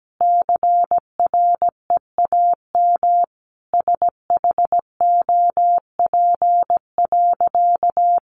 Morse code, heard as a keyed tone, about 17 wpm, 700 Hz. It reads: CREAM SHOP.
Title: Additional Easy Phrases 17wpm